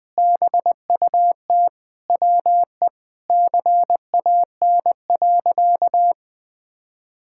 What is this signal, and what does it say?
Morse code, heard as a keyed tone, about 19 wpm, 700 Hz. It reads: BUT WE CAN.
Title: Sentences with 2-Word Phrases - Spoken Beginning - 20wpm